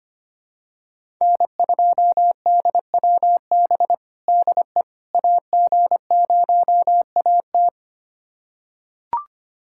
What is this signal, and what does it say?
Morse code, heard as a keyed tone, about 25 wpm, 700 Hz. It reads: N2DWB DE AG0AT
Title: USA Callsign DE Callsign - 5 Characters - 25wpm